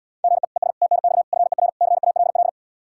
Morse code, heard as a keyed tone, about 79 wpm, 700 Hz. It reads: GE ES TNX FER CALL